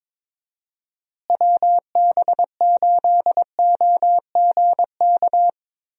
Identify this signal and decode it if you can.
Morse code, heard as a keyed tone, about 22 wpm, 700 Hz. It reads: WB8OGK
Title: USA Call Signs - 6 Characters - Repeated 3x - 22wpm